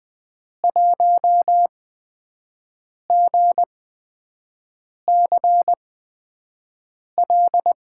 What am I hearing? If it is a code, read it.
Morse code, heard as a keyed tone, about 20 wpm, spaced as for 5 wpm, 700 Hz. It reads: 1GCL
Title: USA Call Signs - 8x Character Spacing - 20wpm